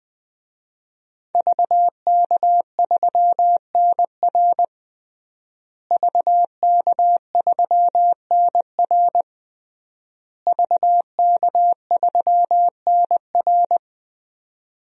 Morse code, heard as a keyed tone, about 20 wpm, 700 Hz. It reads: VK3NR VK3NR VK3NR
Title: DXCC-100 Call Signs - 5 Characters - Repeated 3x - 20wpm